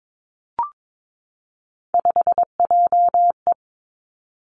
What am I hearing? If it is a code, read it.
Morse code, heard as a keyed tone, about 22 wpm, 700 Hz. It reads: 5JE
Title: Three Challenging Letter-Number Combinations 22wpm